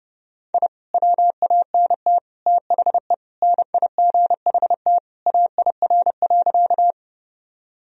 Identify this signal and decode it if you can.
Morse code, heard as a keyed tone, about 30 wpm, 700 Hz. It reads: I WANT THE NIGHT AIR.
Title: Sentences from Top 1000 Words - Review Part 1 - 30wpm